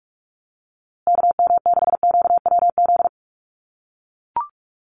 Morse code, heard as a keyed tone, about 44 wpm, 700 Hz. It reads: KM6QWZ
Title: USA Call Signs Repeated 2x 45wpm